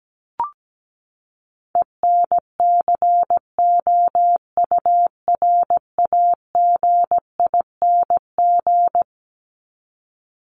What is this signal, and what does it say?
Morse code, heard as a keyed tone, about 17 wpm, 700 Hz. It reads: ENCOURAGING
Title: Top 1751-2000 Words 17wpm